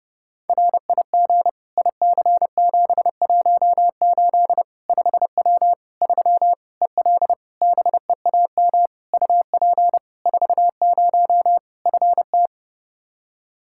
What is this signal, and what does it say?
Morse code, heard as a keyed tone, about 32 wpm, 700 Hz. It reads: RIG IC718 5W 3 EL BEAM UP 40 FT
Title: K1USN SST QSO Practice 30wpm